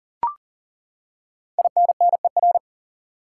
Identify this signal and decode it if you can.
Morse code, heard as a keyed tone, about 40 wpm, 700 Hz. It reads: INNER
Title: Five Letter Words - 40wpm